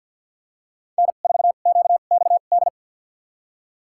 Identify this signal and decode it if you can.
Morse code, heard as a keyed tone, about 51 wpm, spaced as for 36 wpm, 700 Hz. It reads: N4QXD
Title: USA Call Signs - 5 Characters - 2x Character Spacing - 50wpm